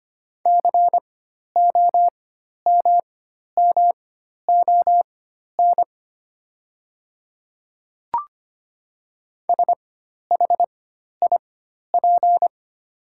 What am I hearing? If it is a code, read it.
Morse code, heard as a keyed tone, about 25 wpm, spaced as for 12 wpm, 690 Hz. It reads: COMMON SHIP
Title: 4x Character Spacing - Top 500 Words - 25wpm